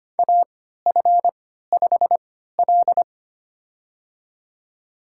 Morse code, heard as a keyed tone, about 25 wpm, spaced as for 14 wpm, 700 Hz. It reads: AF5L
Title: USA Call Signs - 4 Characters - 3x Character Spacing - 25wpm